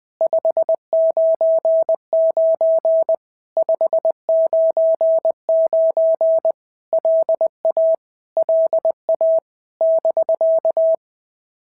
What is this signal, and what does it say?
Morse code, heard as a keyed tone, about 20 wpm, 650 Hz. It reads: 599 599 LA LA <BK>
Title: POTA QSO Practice - 20wpm